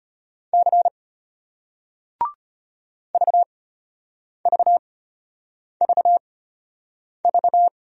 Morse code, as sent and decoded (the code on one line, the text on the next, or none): -.-. / ...- ...- ...- ...-
C VVVV